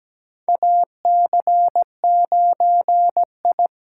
Morse code, heard as a keyed tone, about 17 wpm, 700 Hz. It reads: AC9I
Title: USA Call Signs - 5 Characters - 8x Character Spacing - 17wpm